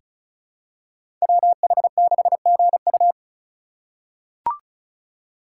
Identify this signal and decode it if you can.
Morse code, heard as a keyed tone, about 34 wpm, 700 Hz. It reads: WH6GU
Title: USA Call Signs - 5 Characters - 3x Character Spacing - 35wpm